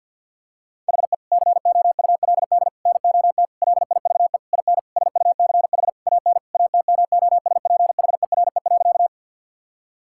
Morse code, heard as a keyed tone, about 50 wpm, 700 Hz. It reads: HE COULD NOT LIVE IN SUCH AN ATMOSPHERE.
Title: Sentences from Top 1600 Words 50wpm